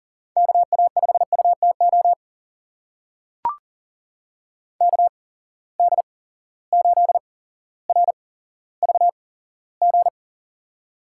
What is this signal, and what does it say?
Morse code, heard as a keyed tone, about 40 wpm, spaced as for 17 wpm, 700 Hz. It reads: KA5UTO KD8RVG
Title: USA Call Signs - 6 Characters - 8x Character Spacing - 40wpm